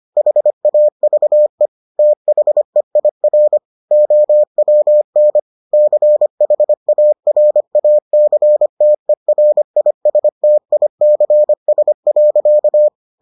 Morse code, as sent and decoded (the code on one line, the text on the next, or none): .... .- ...- . / - .... .. .-. / --- .-- -. / -.-. .... .- .-. .- -.-. - . .-. .. ... - .. -.-. ... .-.-.-
HAVE THEIR OWN CHARACTERISTICS.